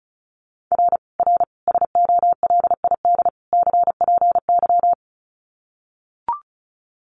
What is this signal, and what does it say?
Morse code, heard as a keyed tone, about 34 wpm, 700 Hz. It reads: R R SOLID CPY
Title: Common QSO Phrases 35wpm